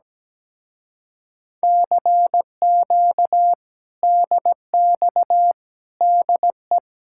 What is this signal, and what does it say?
Morse code, heard as a keyed tone, about 17 wpm, 700 Hz. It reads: CQ DX DE
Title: Speed-Racing Common QSO Phrases Encore - 17wpm